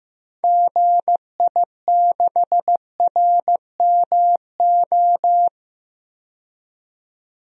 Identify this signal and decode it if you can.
Morse code, heard as a keyed tone, about 15 wpm, 700 Hz. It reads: GI6RMO
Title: DXCC-100 Call Signs - 6 Characters - 15wpm